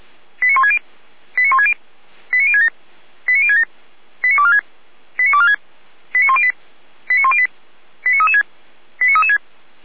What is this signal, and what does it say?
Five-tone selective call (ZVEI), heard as quick runs of five tones, tones of about 70 ms. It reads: 89160 89160 89067 89067 89246 89246 89108 89108 89336 89336
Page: /455/SDR/ZVEI/